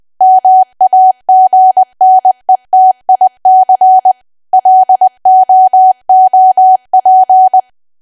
Morse code, heard as a keyed tone, about 20 wpm, 750 Hz. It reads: MAGNETIC LOOP
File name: cwa / W1AW-Level3 / 20WPM